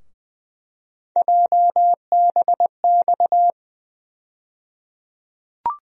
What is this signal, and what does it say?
Morse code, heard as a keyed tone, about 21 wpm, 700 Hz. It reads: JBX
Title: USA Call Sign Postfixes - 3 Letters 20wpm